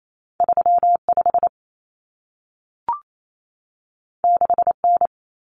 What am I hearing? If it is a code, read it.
Morse code, heard as a keyed tone, about 28 wpm, 700 Hz. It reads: 35 6N
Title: Common Cut Numbers - 2-Digit 28wpm